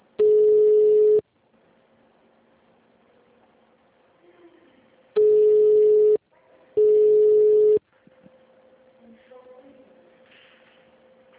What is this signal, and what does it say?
Ringback tone (Europe, 425 Hz).
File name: out- -2003- - - .45